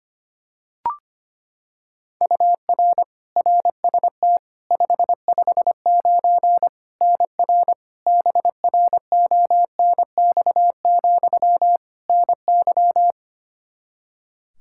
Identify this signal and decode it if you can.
Morse code, heard as a keyed tone, about 25 wpm, 700 Hz. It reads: UR RST 559 NR BRONX, NY
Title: Practice QSO - Exchange 1 - Beginning - 25wpm